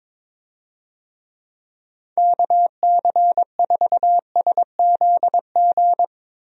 Morse code, heard as a keyed tone, about 22 wpm, 700 Hz. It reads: KC4SZG
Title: USA Call Signs - 6 Characters - 22wpm